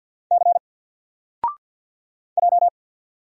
Morse code, heard as a keyed tone, about 50 wpm, 700 Hz. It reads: C J
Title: Single Character (Letter:Number:Punctuation:Prosign) 50wpm